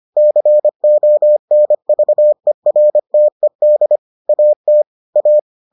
Morse code, heard as a keyed tone, about 25 wpm, 600 Hz. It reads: CONVERTED AT A